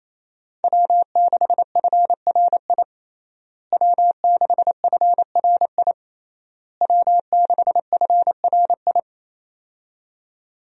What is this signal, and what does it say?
Morse code, heard as a keyed tone, about 27 wpm, 700 Hz. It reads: W6FRI W6FRI W6FRI
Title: USA Call Signs - 5 Characters - Repeated 3x - 28wpm